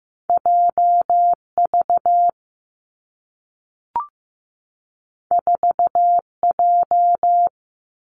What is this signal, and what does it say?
Morse code, heard as a keyed tone, about 15 wpm, 700 Hz. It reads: J V 4 J
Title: Two Challenging Letter-Number Combinations 15wpm